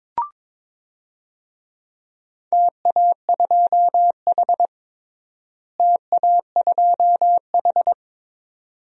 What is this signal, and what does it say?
Morse code, heard as a keyed tone, about 22 wpm, 700 Hz. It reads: TA2H TA2H
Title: DXCC-100 Call Signs - 4 Characters - Repeated 2x - 22wpm